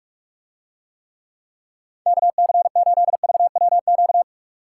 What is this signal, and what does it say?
Morse code, heard as a keyed tone, about 49 wpm, 700 Hz. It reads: KC8VWQ